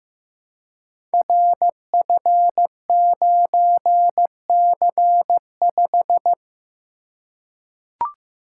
Morse code, heard as a keyed tone, about 15 wpm, 700 Hz. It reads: RF9C5
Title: Code Groups - 5 Letter:Numbers - 15wpm